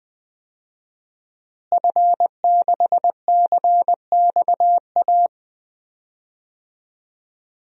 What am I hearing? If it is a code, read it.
Morse code, heard as a keyed tone, about 20 wpm, 700 Hz. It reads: F6CXA
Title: DXCC-100 Call Signs - 5 Characters - Repeated 3x - 20wpm